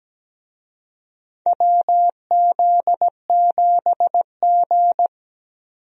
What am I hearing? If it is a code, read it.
Morse code, heard as a keyed tone, about 17 wpm, 700 Hz. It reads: WZ7G
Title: USA Call Signs - 4 Characters - 17wpm